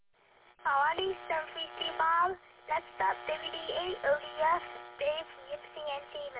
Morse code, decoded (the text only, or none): none